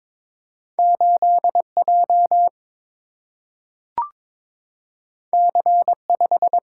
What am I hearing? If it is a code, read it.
Morse code, heard as a keyed tone, about 21 wpm, 700 Hz. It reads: 8 J C 5
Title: Single-Letter Single-Number 22wpm